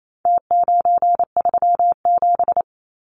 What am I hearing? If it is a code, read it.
Morse code, heard as a keyed tone, about 28 wpm, 700 Hz. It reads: T937
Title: Numbers - Contest Style - 4-Digit 28wpm